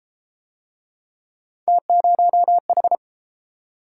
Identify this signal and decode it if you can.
Morse code, heard as a keyed tone, about 33 wpm, 700 Hz. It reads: T0H